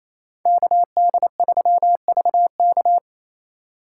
Morse code, heard as a keyed tone, about 34 wpm, 700 Hz. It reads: KD3VK
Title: Speed Racing - USA 2x2 Call Signs 28wpm